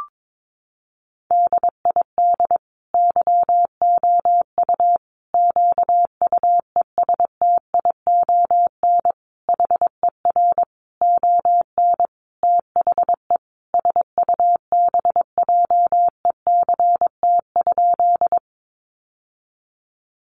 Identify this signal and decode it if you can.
Morse code, heard as a keyed tone, about 22 wpm, 700 Hz. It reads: DID YOU QUESTION HER ON THE SUBJECT?
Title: Sentences from Top 800 Words 22wpm